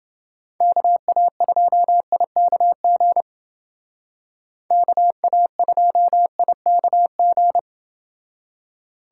Morse code, heard as a keyed tone, about 28 wpm, 700 Hz. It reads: KA2IKG KA2IKG